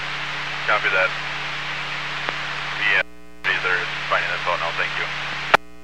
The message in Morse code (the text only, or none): none